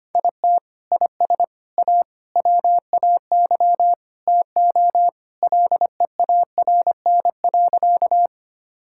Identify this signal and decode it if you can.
Morse code, heard as a keyed tone, about 25 wpm, 700 Hz. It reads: IT IS A WAY TO LEARN.
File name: practice / 25wpm / Sentences with 3-Word Phrases - 25wpm